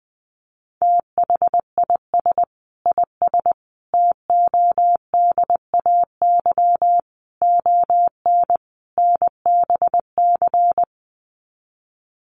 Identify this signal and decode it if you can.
Morse code, heard as a keyed tone, about 19 wpm, 700 Hz. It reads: THIS IS TODAY ON NBC
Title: Sets of 5 Words Encore 20wpm